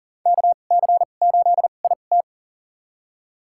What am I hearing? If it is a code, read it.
Morse code, heard as a keyed tone, about 40 wpm, spaced as for 29 wpm, 700 Hz. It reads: KC8IT
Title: USA Call Signs - 2x Character Spacing - 40wpm